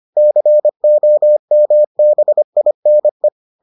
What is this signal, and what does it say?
Morse code, heard as a keyed tone, about 25 wpm, 600 Hz. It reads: COMBINE